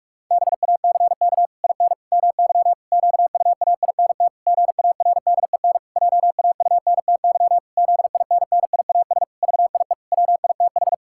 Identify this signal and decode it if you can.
Morse code, heard as a keyed tone, about 46 wpm, 700 Hz. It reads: BACK IN MY QUAINT GARDEN JAUNTY ZINNIAS VIE WITH
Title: Pangram Challenge v2 - 45wpm